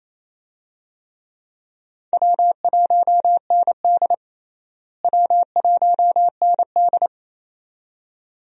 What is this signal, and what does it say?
Morse code, heard as a keyed tone, about 28 wpm, 700 Hz. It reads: W1ND W1ND
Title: USA Call Signs - 4 Characters - Repeated 2x - 28wpm